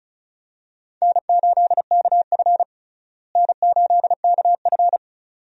Morse code, heard as a keyed tone, about 34 wpm, 700 Hz. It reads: N8KF N8KF
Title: USA Call Signs - 4 Characters - Repeated 2x - 35wpm